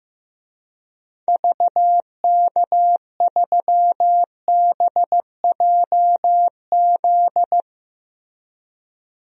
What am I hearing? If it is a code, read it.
Morse code, heard as a keyed tone, about 15 wpm, 700 Hz. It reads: VK3BJZ
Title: DXCC-100 Call Signs - 6 Characters - Repeated 3x - 15wpm